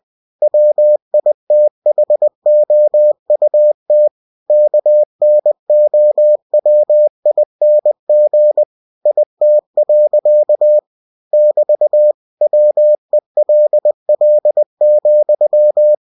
Morse code, heard as a keyed tone, about 20 wpm, 600 Hz. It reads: WITHOUT KNOWING IT. <BT> WELL,
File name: cw-books / winnie-the-pooh / winnie-20wpm-page-0030